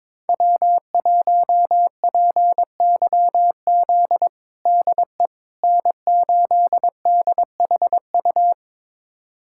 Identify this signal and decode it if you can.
Morse code, heard as a keyed tone, about 22 wpm, 700 Hz. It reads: W1PYZ DE N8DHU
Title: USA Callsign DE Callsign - 5 Characters - 22wpm